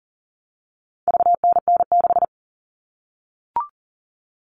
Morse code, heard as a keyed tone, about 40 wpm, 700 Hz. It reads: VNN6
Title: All Cut Numbers - 4-Digit 40wpm